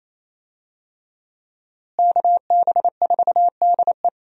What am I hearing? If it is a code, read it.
Morse code, heard as a keyed tone, about 27 wpm, 700 Hz. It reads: KB4DE
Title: USA Call Signs - 5 Characters - Repeated 3x - 28wpm